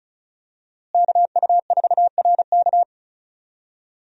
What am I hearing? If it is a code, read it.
Morse code, heard as a keyed tone, about 36 wpm, 700 Hz. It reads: KU4RK
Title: USA Call Signs - 5 Characters - Repeated 2x - 35wpm